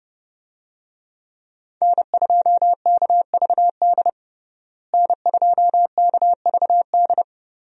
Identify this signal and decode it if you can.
Morse code, heard as a keyed tone, about 32 wpm, 700 Hz. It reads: N2KVD N2KVD
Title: USA Call Signs - 5 Characters - Repeated 2x - 30wpm